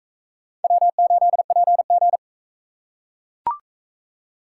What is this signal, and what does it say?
Morse code, heard as a keyed tone, about 37 wpm, 700 Hz. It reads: W8PG